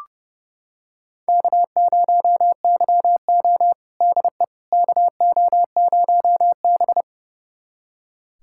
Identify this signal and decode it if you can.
Morse code, heard as a keyed tone, about 29 wpm, 700 Hz. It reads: K0YO DE KO0B
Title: USA Callsign DE Callsign - 4 Characters - 30wpm